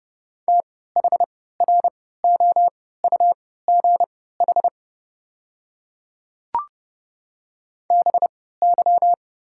Morse code, heard as a keyed tone, about 31 wpm, spaced as for 17 wpm, 690 Hz. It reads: THROUGH BY
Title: 3x Character Spacing - Top 300 Words - 30wpm